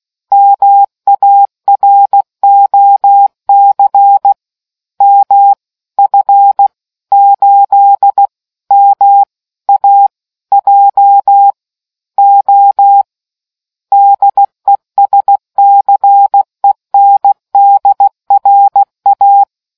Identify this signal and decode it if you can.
Morse code, heard as a keyed tone, about 16 wpm, 800 Hz. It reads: MAROC M F 8 M A J O DESCENDRA